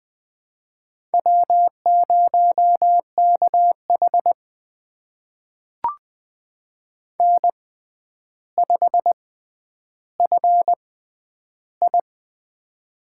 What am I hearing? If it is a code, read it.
Morse code, heard as a keyed tone, about 20 wpm, spaced as for 10 wpm, 700 Hz. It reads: W0KH N5FI
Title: USA Call Signs - 4 Characters - 6x Character Spacing - 20wpm